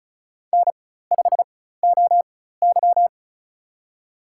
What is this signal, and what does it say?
Morse code, heard as a keyed tone, about 36 wpm, spaced as for 16 wpm, 700 Hz. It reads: N5OY
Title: USA Call Signs - 4 Characters - 4x Character Spacing - 35wpm